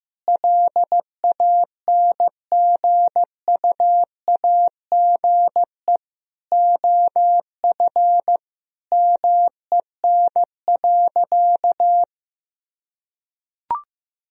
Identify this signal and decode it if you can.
Morse code, heard as a keyed tone, about 15 wpm, 700 Hz. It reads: LANGUAGE OF MEN.
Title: Sentences from Top 1000 Words - Review Part 3 - 15wpm